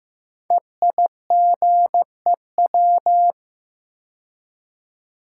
Morse code, heard as a keyed tone, about 15 wpm, 700 Hz. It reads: EIGEW
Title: Code Groups - 5 Letters - 15wpm